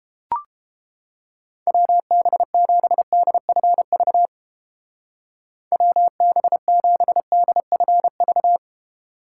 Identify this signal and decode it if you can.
Morse code, heard as a keyed tone, about 31 wpm, 700 Hz. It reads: WB7DFV WB7DFV